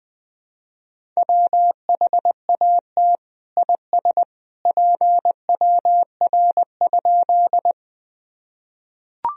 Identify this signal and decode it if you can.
Morse code, heard as a keyed tone, about 20 wpm, 700 Hz. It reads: WHAT IS PWR?
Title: Sentences from Top 500 Words and Abbreviations 20wpm